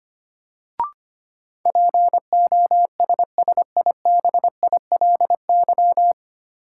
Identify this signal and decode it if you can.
Morse code, heard as a keyed tone, about 25 wpm, 700 Hz. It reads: POSSIBILY
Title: Sentences from Top 1300 Words - New Word Drill Alt - 25wpm